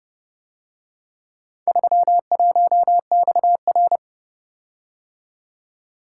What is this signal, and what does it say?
Morse code, heard as a keyed tone, about 30 wpm, 700 Hz. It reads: 31XR